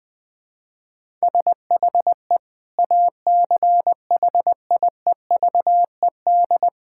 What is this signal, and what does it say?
Morse code, heard as a keyed tone, about 20 wpm, 700 Hz. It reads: SHE ACHIEVED
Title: Sentences with 7-Word Phrases - 20wpm